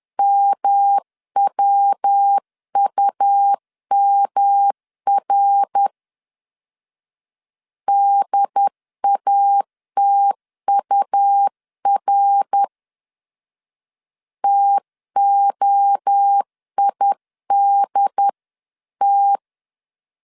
Morse code, heard as a keyed tone, about 10 wpm, 800 Hz. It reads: MWUMR DATUR TOIDT
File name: 10wpm